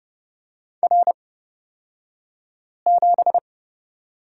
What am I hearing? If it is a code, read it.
Morse code, heard as a keyed tone, about 30 wpm, spaced as for 5 wpm, 700 Hz. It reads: R7
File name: Lesson 36d - BT - Two Characters 30wpm